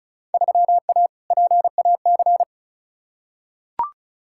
Morse code, heard as a keyed tone, about 36 wpm, 700 Hz. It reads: 3A PAC
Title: Field Day Exchanges Encore - 35wpm